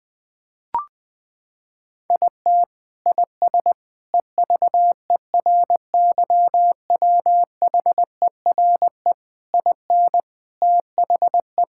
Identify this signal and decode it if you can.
Morse code, heard as a keyed tone, about 19 wpm, 700 Hz. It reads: IT IS EVERYWHERE IN THE
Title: Sentences from Top 1600 Words 20wpm